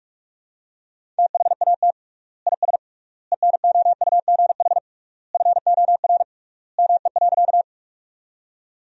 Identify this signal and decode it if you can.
Morse code, heard as a keyed tone, about 42 wpm, 700 Hz. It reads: THAT IS ENOUGH FOR ME.
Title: 3x Word-Spacing - Sentences from Top 400 Words - 45wpm